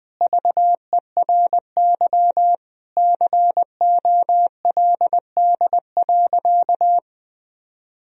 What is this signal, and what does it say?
Morse code, heard as a keyed tone, about 20 wpm, 700 Hz. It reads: VERY COLD.
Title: Sentences from Top 1000 Words - Review Part 2 - 20wpm